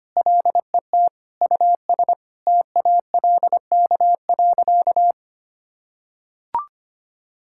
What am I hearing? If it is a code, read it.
Morse code, heard as a keyed tone, about 25 wpm, 700 Hz. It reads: LET US TALK.
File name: practice / 25wpm / Sentences with 4-Word Phrases - 25wpm